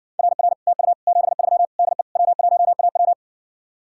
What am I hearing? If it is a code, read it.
Morse code, heard as a keyed tone, about 70 wpm, 700 Hz. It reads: RR TU 73 DE W1AW